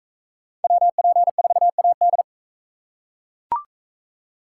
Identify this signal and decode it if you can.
Morse code, heard as a keyed tone, about 43 wpm, 700 Hz. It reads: WP4AD